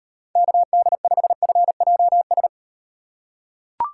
Morse code, heard as a keyed tone, about 38 wpm, 700 Hz. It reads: KD5FJS